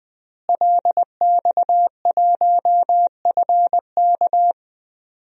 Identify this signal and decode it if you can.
Morse code, heard as a keyed tone, about 21 wpm, 700 Hz. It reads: LX1FK
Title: DXCC-100 Call Signs - 5 Characters - Repeated 2x - 20wpm